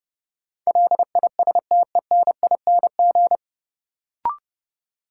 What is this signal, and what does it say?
Morse code, heard as a keyed tone, about 30 wpm, 700 Hz. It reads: LISTENING
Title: Sets of 5 Words - New Word Drill - 30wpm